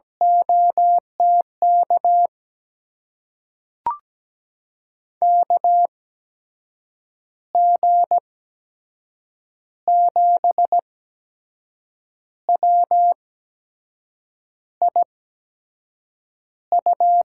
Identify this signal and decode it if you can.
Morse code, heard as a keyed tone, about 17 wpm, spaced as for 6 wpm, 700 Hz. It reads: OTK KG7WIU
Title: USA Call Signs - 6 Characters - 8x Character Spacing - 17wpm